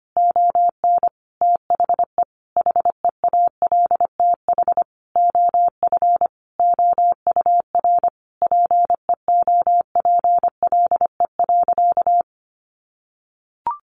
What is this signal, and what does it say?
Morse code, heard as a keyed tone, about 25 wpm, 700 Hz. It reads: ON THE HEALTH OF OUR PEOPLE.
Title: Sentences from Top 1500 Words 25wpm